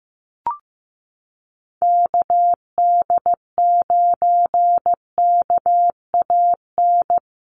Morse code, heard as a keyed tone, about 15 wpm, 700 Hz. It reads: KD9KAN
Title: USA Call Signs Repeated 3x 15wpm